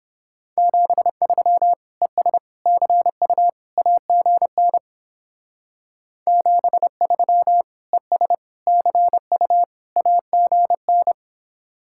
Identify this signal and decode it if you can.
Morse code, heard as a keyed tone, about 29 wpm, 700 Hz. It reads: 73 ES CU AGN 73 ES CU AGN